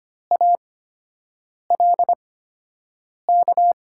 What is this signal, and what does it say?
Morse code, heard as a keyed tone, about 25 wpm, spaced as for 7 wpm, 700 Hz. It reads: ALK
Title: 8x Character Spacing - Top 300 Words - 25wpm